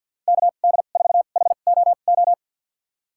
Morse code, heard as a keyed tone, about 50 wpm, spaced as for 36 wpm, 700 Hz. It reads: KD4HOO
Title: USA Call Signs - 2x Character Spacing - 50wpm